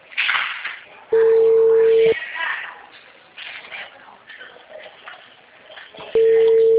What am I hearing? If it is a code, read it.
Telephone call progress tone (China, 450 Hz), heard as a telephone network's tone, about 450 Hz, with speech. Ringback tone.